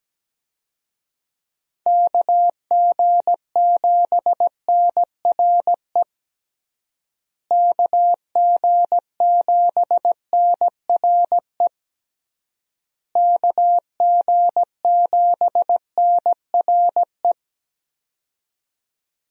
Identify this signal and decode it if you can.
Morse code, heard as a keyed tone, about 17 wpm, 700 Hz. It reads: KG7NRE KG7NRE KG7NRE
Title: USA Call Signs - 6 Characters - Repeated 3x - 17wpm